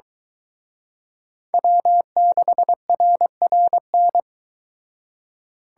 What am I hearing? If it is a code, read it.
Morse code, heard as a keyed tone, about 22 wpm, 700 Hz. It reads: W6RRN